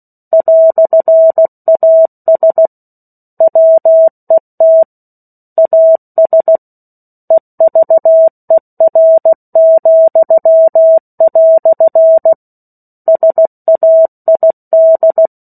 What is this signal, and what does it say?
Morse code, heard as a keyed tone, about 16 wpm, 650 Hz. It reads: "AS WET AS EVER," SAID
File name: 03-Chapter-03 16 WPM